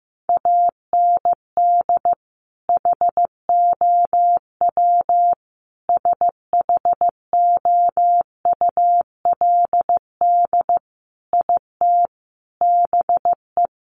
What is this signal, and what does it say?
Morse code, heard as a keyed tone, about 15 wpm, 700 Hz. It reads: AND HOW SHOULD IT BE
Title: Sentences from Top 1200 Words 15wpm